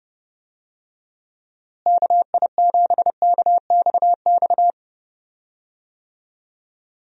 Morse code, heard as a keyed tone, about 30 wpm, 700 Hz. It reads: KI7KXX
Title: USA Call Signs - 6 Characters - Repeated 3x - 30wpm